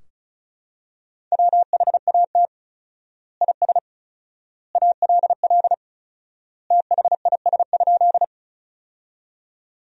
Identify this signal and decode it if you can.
Morse code, heard as a keyed tone, about 36 wpm, 700 Hz. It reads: WHAT IS ALL THIS?
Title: 4x Word-Spacing - Sentences from Top 100 Words - 35wpm